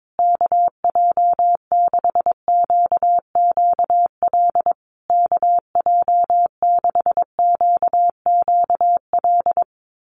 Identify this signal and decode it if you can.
Morse code, heard as a keyed tone, about 22 wpm, 700 Hz. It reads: KJ6QQL KJ6QQL
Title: USA Call Signs Repeated 3x 22wpm